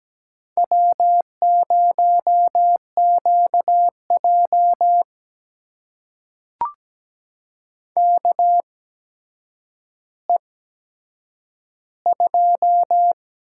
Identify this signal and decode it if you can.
Morse code, heard as a keyed tone, about 17 wpm, spaced as for 8 wpm, 700 Hz. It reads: W0QJ KE2
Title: USA Call Signs - 4 Characters - 8x Character Spacing - 17wpm